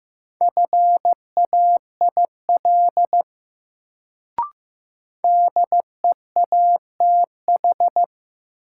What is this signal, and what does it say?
Morse code, heard as a keyed tone, about 15 wpm, 700 Hz. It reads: FAIL DEATH
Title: Sentences from Top 1100 Words - New Word Drill Alt - 15wpm